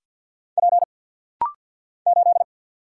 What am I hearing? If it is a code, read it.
Morse code, heard as a keyed tone, about 50 wpm, 700 Hz. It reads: P 8
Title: Single Letter-Number - Rapid-Fire 50wpm